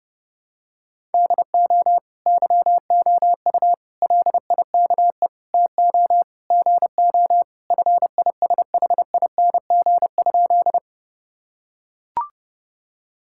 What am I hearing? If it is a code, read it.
Morse code, heard as a keyed tone, about 30 wpm, 700 Hz. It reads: DO YOU LIKE TO GO FISHING?
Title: Sentences from Top 1000 Words - Review Part 5 - 30wpm